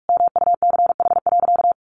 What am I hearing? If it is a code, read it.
Morse code, heard as a keyed tone, about 44 wpm, 700 Hz. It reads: MUCH.
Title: Sentences from Top 400 Words 45wpm